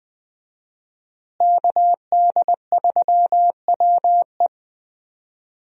Morse code, heard as a keyed tone, about 21 wpm, 700 Hz. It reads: KD3WE
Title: USA Call Signs - 5 Characters - Repeated 3x - 20wpm